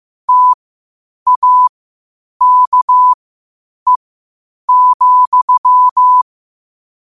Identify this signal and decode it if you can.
Morse code, heard as a keyed tone, about 15 wpm, spaced as for 8 wpm, 1000 Hz. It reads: TAKE,
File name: qotdfiles / 8wpm